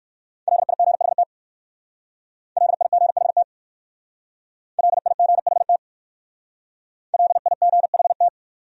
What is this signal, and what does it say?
Morse code, heard as a keyed tone, about 56 wpm, 700 Hz. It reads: LIGHT LIGHT LIGHT LIGHT
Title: Speed-Racing Top 200 Words 45wpm